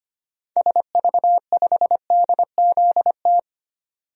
Morse code, heard as a keyed tone, about 25 wpm, 700 Hz. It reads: SV5DZT
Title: DXCC-100 Call Signs - 6 Characters - 25wpm